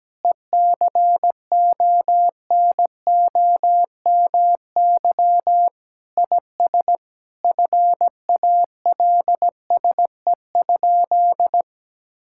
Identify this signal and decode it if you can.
Morse code, heard as a keyed tone, about 17 wpm, 700 Hz. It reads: ECONOMY IS FALSE?
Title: Sentences from Top 1600 Words 17wpm